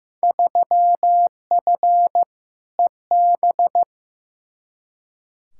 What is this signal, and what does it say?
Morse code, heard as a keyed tone, about 15 wpm, 700 Hz. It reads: 3F EB
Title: Field Day Exchanges - 15wpm